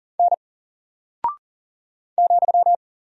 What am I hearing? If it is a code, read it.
Morse code, heard as a keyed tone, about 38 wpm, 700 Hz. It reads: N ,
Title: Single Character (Letter:Number:Punctuation:Prosign) Alt 40wpm